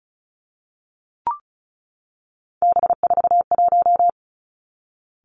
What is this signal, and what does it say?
Morse code, heard as a keyed tone, about 36 wpm, 700 Hz. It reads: B41
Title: Three Challenging Letter-Number Combinations 35wpm